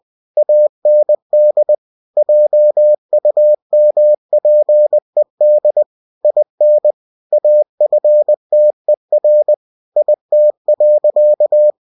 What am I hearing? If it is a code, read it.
Morse code, heard as a keyed tone, about 20 wpm, 600 Hz. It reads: AND JUMPED IN AFTER IT.
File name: cw-books / winnie-the-pooh / winnie-20wpm-page-0095